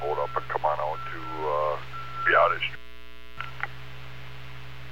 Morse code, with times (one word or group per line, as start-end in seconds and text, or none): none